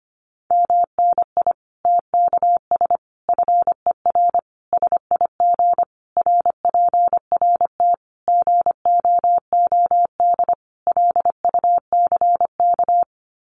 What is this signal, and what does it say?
Morse code, heard as a keyed tone, about 25 wpm, 700 Hz. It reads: MNI TKS FER SIG RPRT GOOD LUCK
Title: Common QSO Phrases 25wpm